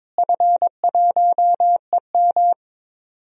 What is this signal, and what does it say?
Morse code, heard as a keyed tone, about 22 wpm, 700 Hz. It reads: F1EM
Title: DXCC-100 Call Signs - 4 Characters - Repeated 3x - 22wpm